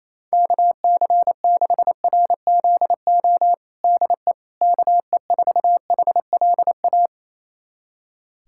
Morse code, heard as a keyed tone, about 28 wpm, 700 Hz. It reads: KC6RZO DE KE4HLA
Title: USA Callsign DE Callsign - 6 Characters - 28wpm